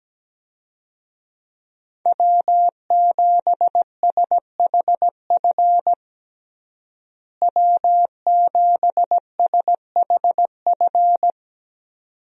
Morse code, heard as a keyed tone, about 17 wpm, 700 Hz. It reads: W7SHF W7SHF
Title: USA Call Signs - 5 Characters - Repeated 2x - 17wpm